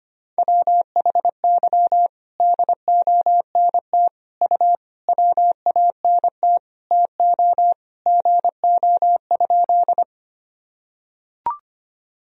Morse code, heard as a keyed tone, about 25 wpm, 700 Hz. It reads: WHY DONT U WANT TO GO?
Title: Sentences from Top 300 Words and Abbreviations 25wpm